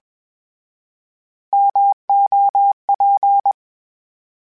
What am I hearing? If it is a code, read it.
Morse code, heard as a keyed tone, about 21 wpm, 800 Hz. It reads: MOP
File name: Misc / Misc-mp3s / Misc @20